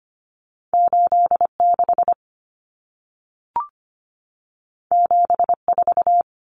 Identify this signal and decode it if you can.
Morse code, heard as a keyed tone, about 25 wpm, 700 Hz. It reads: 86 74
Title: Numbers - 2-Digit 25wpm